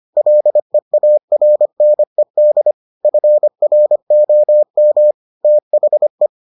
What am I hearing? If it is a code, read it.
Morse code, heard as a keyed tone, about 25 wpm, 600 Hz. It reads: LEARNED FROM THE